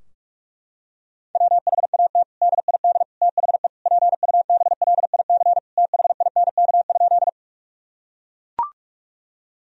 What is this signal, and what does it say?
Morse code, heard as a keyed tone, about 42 wpm, 700 Hz. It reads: WHAT DID THE PUBLIC THINK?
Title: Sentences from Top 1000 Words - Review Part 3 - 45wpm